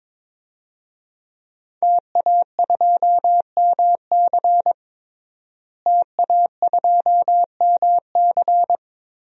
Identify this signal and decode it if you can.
Morse code, heard as a keyed tone, about 22 wpm, 700 Hz. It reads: TA2MC TA2MC
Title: DXCC-100 Call Signs - 5 Characters - Repeated 3x - 22wpm